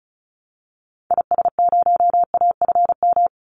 Morse code, heard as a keyed tone, about 34 wpm, 700 Hz. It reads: IS0AFM
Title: Call Signs - 35wpm